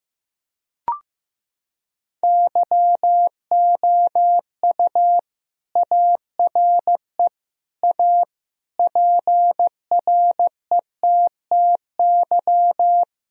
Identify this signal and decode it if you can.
Morse code, heard as a keyed tone, about 15 wpm, 700 Hz. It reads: YOU ARE A PRETTY
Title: Sentences from Top 1000 Words - Review Part 2 - 15wpm